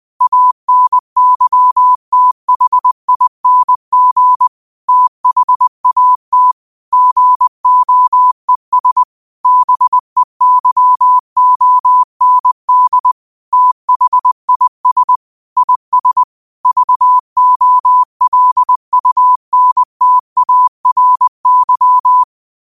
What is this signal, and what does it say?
Morse code, heard as a keyed tone, about 20 wpm, 1000 Hz. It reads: ANYTHING THAT GOES BEYOND THIS IS VOLUNTARY
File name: qotdfiles / 20wpm